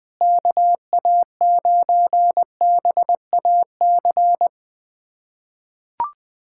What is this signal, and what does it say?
Morse code, heard as a keyed tone, about 20 wpm, 700 Hz. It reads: KA9BAC
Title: USA Callsign DE Callsign - 6 Characters - 20wpm